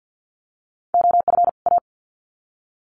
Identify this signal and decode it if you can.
Morse code, heard as a keyed tone, about 50 wpm, 700 Hz. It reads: OF A
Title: Sets of 2 Words 50wpm